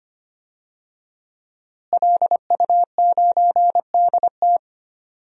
Morse code, heard as a keyed tone, about 25 wpm, 700 Hz. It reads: LU9DT
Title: DXCC-100 Call Signs - 5 Characters - Repeated 2x - 25wpm